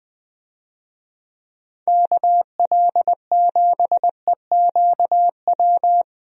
Morse code, heard as a keyed tone, about 20 wpm, 700 Hz. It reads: KL7EQW
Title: DXCC-100 Call Signs - 6 Characters - Repeated 3x - 20wpm